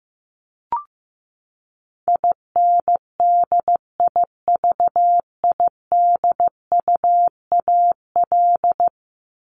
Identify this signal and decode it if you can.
Morse code, heard as a keyed tone, about 15 wpm, 700 Hz. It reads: INDIVIDUAL
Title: ES:S Suffix - Top 100 Words with Base Words 15wpm